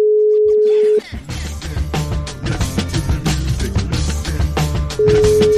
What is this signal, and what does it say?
Ringback tone (Europe, 425 Hz).